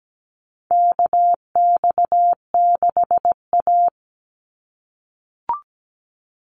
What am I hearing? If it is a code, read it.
Morse code, heard as a keyed tone, about 17 wpm, 700 Hz. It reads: KX6A
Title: NA SOTA Chaser Callsigns 17wpm